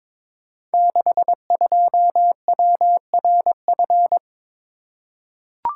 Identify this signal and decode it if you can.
Morse code, heard as a keyed tone, about 22 wpm, 700 Hz. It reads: 62WRF
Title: Code Groups - 5 Letter:Numbers - 22wpm